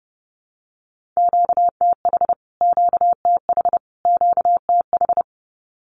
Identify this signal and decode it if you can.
Morse code, heard as a keyed tone, about 30 wpm, 700 Hz. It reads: QTH QTH QTH
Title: Q-Codes Repeated 3x 30wpm